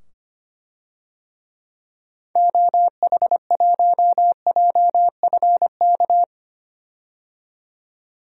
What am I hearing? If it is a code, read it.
Morse code, heard as a keyed tone, about 25 wpm, 700 Hz. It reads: OH1JFK
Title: DXCC-100 Call Signs - 6 Characters - 25wpm